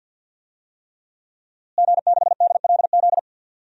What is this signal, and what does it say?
Morse code, heard as a keyed tone, about 47 wpm, 700 Hz. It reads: G6DLZ
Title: DXCC-100 Call Signs - 5 Characters - Repeated 3x - 50wpm